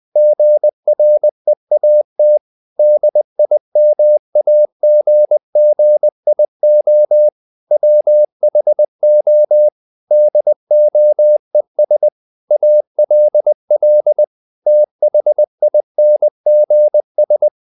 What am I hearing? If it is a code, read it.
Morse code, heard as a keyed tone, about 20 wpm, 600 Hz. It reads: GREAT DIMAGGIO WHO DOES ALL THINGS
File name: the old man and the sea / the old man and the sea part 1 20wpm